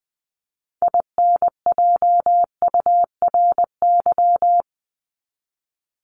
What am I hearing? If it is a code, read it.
Morse code, heard as a keyed tone, about 20 wpm, 700 Hz. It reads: INJURY
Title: Top 1751-2000 Words 20wpm